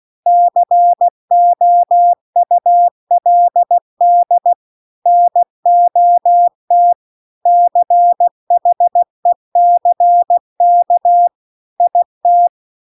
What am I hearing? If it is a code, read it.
Morse code, heard as a keyed tone, about 16 wpm, 700 Hz. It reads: COULD NOT CHECK IT